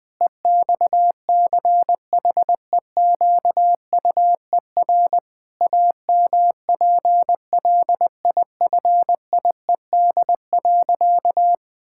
Morse code, heard as a keyed tone, about 20 wpm, 700 Hz. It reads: EXCHEQUER AMPLIFIED.
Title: Pangram Challenge v2 - 20wpm